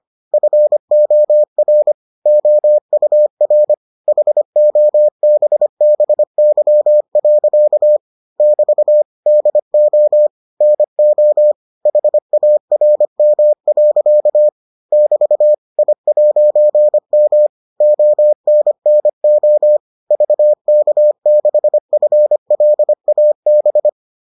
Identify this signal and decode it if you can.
Morse code, heard as a keyed tone, about 25 wpm, 600 Hz. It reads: FOR OUR HOBBY. <BT> DO NO HARM. <BT> I'M ONNO VK6FLAB